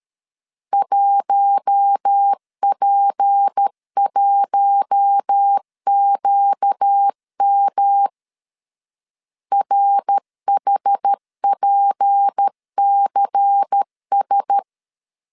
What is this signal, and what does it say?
Morse code, heard as a keyed tone, about 13 wpm, 800 Hz. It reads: 1P1QM RHPCS
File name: LEZIONI / 6 PKX0 / 6 P 12wpm 2033